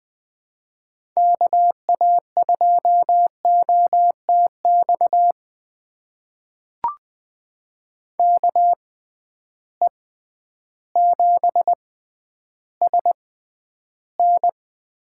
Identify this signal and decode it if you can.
Morse code, heard as a keyed tone, about 20 wpm, spaced as for 11 wpm, 700 Hz. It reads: KA2OTX KE7SN
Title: USA Call Signs - 6 Characters - 6x Character Spacing - 20wpm